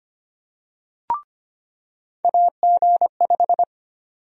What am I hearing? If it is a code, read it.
Morse code, heard as a keyed tone, about 25 wpm, 700 Hz. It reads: AG5
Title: USA Call Sign Prefixes - 25wpm